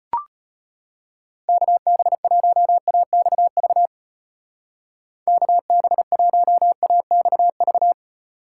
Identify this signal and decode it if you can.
Morse code, heard as a keyed tone, about 32 wpm, 700 Hz. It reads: KB1AXV KB1AXV